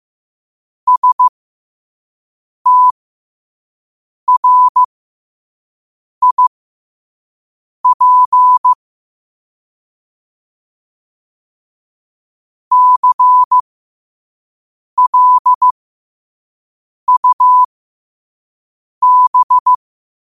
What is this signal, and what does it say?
Morse code, heard as a keyed tone, about 15 wpm, spaced as for 5 wpm, 1000 Hz. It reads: STRIP CLUB
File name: qotdfiles / 5wpm